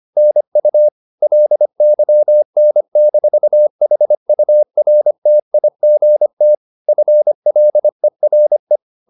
Morse code, heard as a keyed tone, about 25 wpm, 600 Hz. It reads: NU LYN-HURTIGT FLERE
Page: Ferie 25 wpm